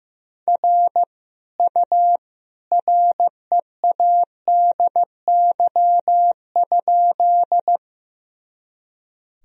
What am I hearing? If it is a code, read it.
Morse code, heard as a keyed tone, about 15 wpm, 700 Hz. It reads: R U READY?
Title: Sentences from Top 400 Words and Abbreviations 15wpm